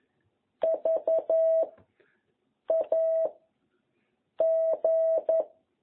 Morse code, heard as a keyed tone, about 11 wpm, 660 Hz. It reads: V A G